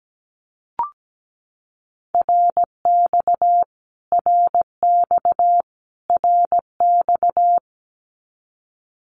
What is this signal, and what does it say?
Morse code, heard as a keyed tone, about 17 wpm, 700 Hz. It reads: RX RX RX
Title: Abbreviations Repeated 3x - 17wpm